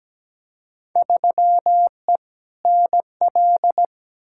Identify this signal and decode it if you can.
Morse code, heard as a keyed tone, about 17 wpm, 700 Hz. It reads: 3E NL
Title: Field Day Exchanges Encore - 17wpm